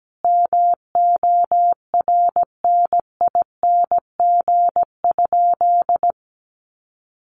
Morse code, heard as a keyed tone, about 17 wpm, 700 Hz. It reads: MORNING?
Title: Sentences from Top 1200 Words 17wpm